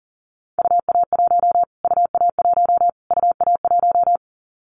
Morse code, heard as a keyed tone, about 40 wpm, 700 Hz. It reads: UA1 UA1 UA1
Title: Callsign Prefixes for Top 25 DXCC - Repeated 3x - 40wpm